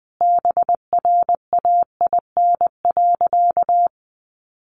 Morse code, heard as a keyed tone, about 20 wpm, 700 Hz. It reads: BRAIN.
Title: Sentences from Top 1100 Words 20wpm